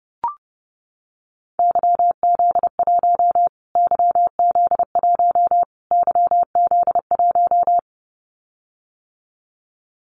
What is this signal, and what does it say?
Morse code, heard as a keyed tone, about 30 wpm, 690 Hz. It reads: YZ1 YZ1 YZ1
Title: Callsign Prefixes for Top 25 DXCC - Repeated 3x - 30wpm